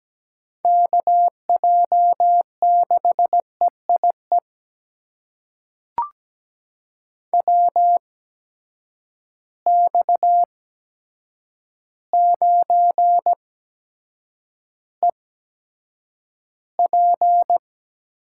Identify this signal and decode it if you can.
Morse code, heard as a keyed tone, about 17 wpm, spaced as for 8 wpm, 700 Hz. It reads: KJ6EIE WX9EP
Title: USA Call Signs - 8x Character Spacing - 17wpm